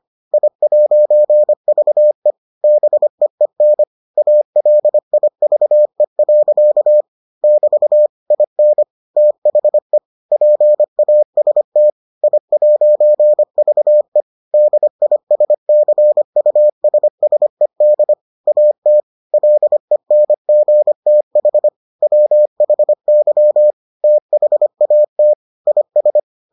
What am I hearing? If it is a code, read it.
Morse code, heard as a keyed tone, about 25 wpm, 600 Hz. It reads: I'VE BEEN ALIVE. <BT> IN THE PAST I'VE DISCUSSED AT LENGTH WHY THAT IS